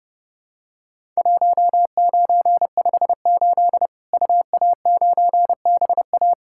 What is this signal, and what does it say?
Morse code, heard as a keyed tone, about 30 wpm, 700 Hz. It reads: 1958 UA9BA
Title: Worked All Europe DX Contest - QTC Exchange - 30wpm